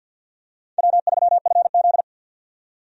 Morse code, heard as a keyed tone, about 53 wpm, 700 Hz. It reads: W3FZ
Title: USA Call Signs - 4 Characters - 50wpm